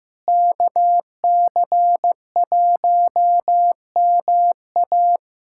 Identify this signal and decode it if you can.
Morse code, heard as a keyed tone, about 16 wpm, 700 Hz. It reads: KC1MA
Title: USA Callsign DE Callsign - 5 Characters - 15wpm